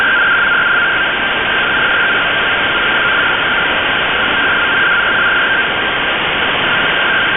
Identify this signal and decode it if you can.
Two-tone teleprinter signal, about 170 Hz shift.
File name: Radio / SDR Audio / gqrx